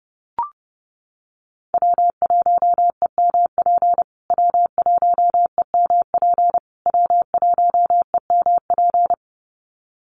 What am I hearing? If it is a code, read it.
Morse code, heard as a keyed tone, about 30 wpm, 700 Hz. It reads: W1EMP W1EMP W1EMP
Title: USA Call Signs Repeated 3x 30wpm